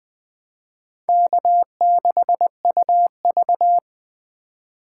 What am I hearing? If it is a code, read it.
Morse code, heard as a keyed tone, about 21 wpm, 700 Hz. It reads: K6UV
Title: Speed Racing - USA 1x2 Call Signs 17wpm